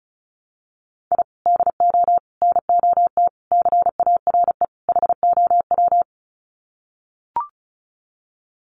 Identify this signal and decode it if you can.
Morse code, heard as a keyed tone, about 34 wpm, 700 Hz. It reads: I DO NOT CARE HOW
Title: Sets of 5 Words Encore2 35wpm